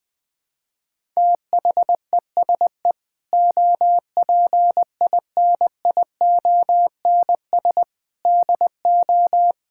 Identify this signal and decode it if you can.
Morse code, heard as a keyed tone, about 20 wpm, 700 Hz. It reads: THESE OPINIONS DO
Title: Sentences with 8-Word Phrases - 20wpm